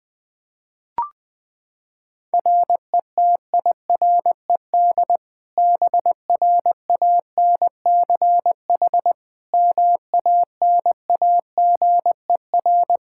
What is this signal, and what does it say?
Morse code, heard as a keyed tone, about 20 wpm, 700 Hz. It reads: RETIRED BRANCH MANAGER
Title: Practice QSO - Exchange 3 - More Info - 20wpm